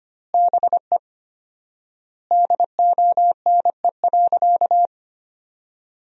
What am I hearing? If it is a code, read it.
Morse code, heard as a keyed tone, about 25 wpm, 700 Hz. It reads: BE DONE.
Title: 4x Word-Spacing - Sentences from Top 500 Words - 25wpm